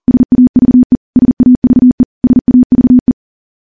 Morse code, heard as a keyed tone, about 40 wpm, 250 Hz. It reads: SAVE SAVE SAVE
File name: morse / morse-training-40WPM